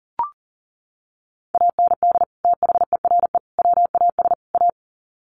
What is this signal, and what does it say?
Morse code, heard as a keyed tone, about 40 wpm, 700 Hz. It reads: AND THERE WAS A
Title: Sets of 4 Words 40wpm